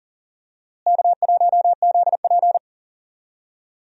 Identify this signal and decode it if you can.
Morse code, heard as a keyed tone, about 37 wpm, 700 Hz. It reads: K1ZP
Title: USA Call Signs - 4 Characters - Repeated 3x - 40wpm